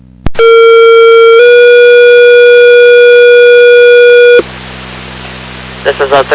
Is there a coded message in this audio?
Yes: two-tone page from 0.4 to 4.4 s, heard as a one-second tone, then a three-second tone, about 460 then 480 Hz, with speech.